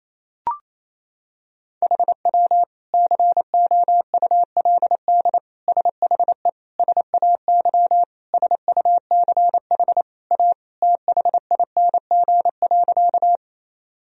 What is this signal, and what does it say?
Morse code, heard as a keyed tone, about 28 wpm, 700 Hz. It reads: HW COULD SHE SAY SUCH A THING.
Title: Sentences from Top 200 Words and Abbreviations 28wpm